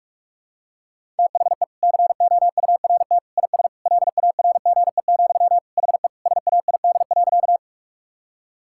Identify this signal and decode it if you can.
Morse code, heard as a keyed tone, about 44 wpm, 700 Hz. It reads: THE COURT IS LARGE, HE SAID.
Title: Sentences from Top 1600 Words 45wpm